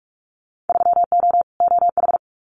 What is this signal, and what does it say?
Morse code, heard as a keyed tone, about 46 wpm, 700 Hz. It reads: 3O OH
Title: Winter Field Day Exchanges 45wpm